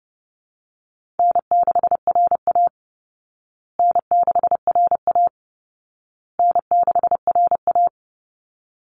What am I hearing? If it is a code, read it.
Morse code, heard as a keyed tone, about 30 wpm, 700 Hz. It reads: N6RA N6RA N6RA
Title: Easy Call Signs Repeated 3x - 30wpm